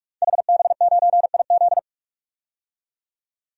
Morse code, heard as a keyed tone, about 44 wpm, 700 Hz. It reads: HB9IZ
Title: DXCC-100 Call Signs - 5 Characters - 45wpm